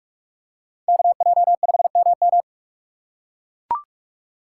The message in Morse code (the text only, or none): KJ5MM